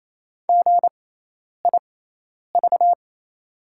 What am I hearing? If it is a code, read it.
Morse code, heard as a keyed tone, about 28 wpm, spaced as for 10 wpm, 690 Hz. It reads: GIV
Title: 6x Character Spacing - Top 500 Words - 28wpm